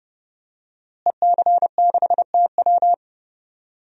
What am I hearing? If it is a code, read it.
Morse code, heard as a keyed tone, about 30 wpm, 700 Hz. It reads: EC6TW
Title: DXCC-100 Call Signs - 5 Characters - Repeated 2x - 30wpm